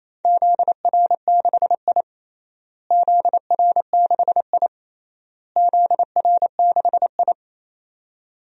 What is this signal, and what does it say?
Morse code, heard as a keyed tone, about 28 wpm, 700 Hz. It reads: ZR6I ZR6I ZR6I
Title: DXCC-100 Call Signs - 4 Characters - Repeated 3x - 28wpm